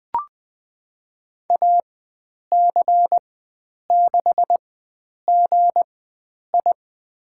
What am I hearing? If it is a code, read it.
Morse code, heard as a keyed tone, about 20 wpm, spaced as for 9 wpm, 700 Hz. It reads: AC6GI
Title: USA Call Signs - 5 Characters - 4x Character Spacing - 20wpm